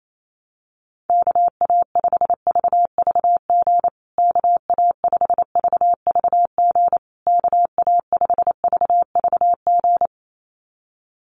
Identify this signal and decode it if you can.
Morse code, heard as a keyed tone, about 28 wpm, 700 Hz. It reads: KA5VVG KA5VVG KA5VVG
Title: USA Call Signs Repeated 3x 28wpm